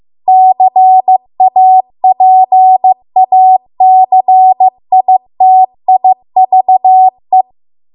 Morse code, heard as a keyed tone, about 15 wpm, 750 Hz. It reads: CAPACITIVE